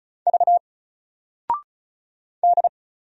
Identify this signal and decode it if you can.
Morse code, heard as a keyed tone, about 36 wpm, 700 Hz. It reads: V D
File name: Single-Letter-Number-35wpm